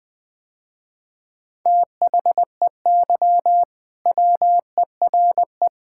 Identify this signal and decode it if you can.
Morse code, heard as a keyed tone, about 20 wpm, 690 Hz. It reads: THEY WERE
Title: Sentences with 4-Word Phrases - Spoken Beginning - 20wpm